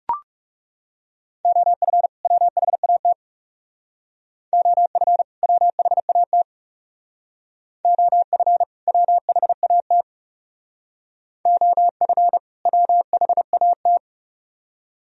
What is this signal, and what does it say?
Morse code, heard as a keyed tone, about 39 wpm, 700 Hz. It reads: OF WHAT OF WHAT OF WHAT OF WHAT